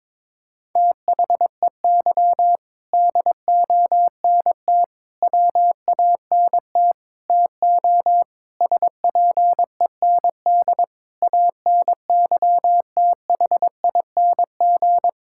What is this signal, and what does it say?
Morse code, heard as a keyed tone, about 22 wpm, 700 Hz. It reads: THEY DONT WANT TO SPEND ANYTHING
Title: Sentences from Top 1000 Words - Review Part 5 - 22wpm